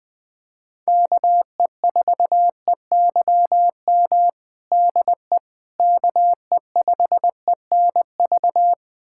Morse code, heard as a keyed tone, about 20 wpm, 700 Hz. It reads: KE4EYM DE KE5ENV
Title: USA Callsign DE Callsign - 6 Characters - 20wpm